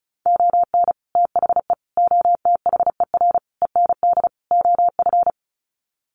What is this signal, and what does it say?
Morse code, heard as a keyed tone, about 34 wpm, 700 Hz. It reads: ON THE OTHER END OF
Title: Sets of 5 Words Encore 35wpm